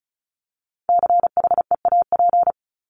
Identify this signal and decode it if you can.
Morse code, heard as a keyed tone, about 34 wpm, 700 Hz. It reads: CHEAP
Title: Top 1001-1250 Words 35wpm